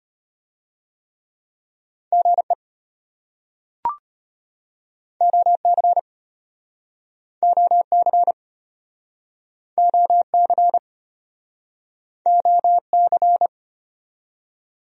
Morse code, heard as a keyed tone, about 36 wpm, 700 Hz. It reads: GE OC OC OC OC